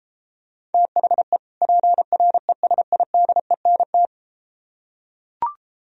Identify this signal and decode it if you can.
Morse code, heard as a keyed tone, about 33 wpm, 700 Hz. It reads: THE PRESIDENT